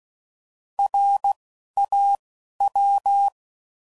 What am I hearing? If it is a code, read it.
Morse code, heard as a keyed tone, about 16 wpm, 800 Hz. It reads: R A W
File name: Prefixes / By First Letter / Rs